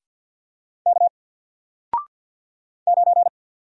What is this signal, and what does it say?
Morse code, heard as a keyed tone, about 50 wpm, 700 Hz. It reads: K 9
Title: Single Character (Letter:Number:Punctuation:Prosign) 50wpm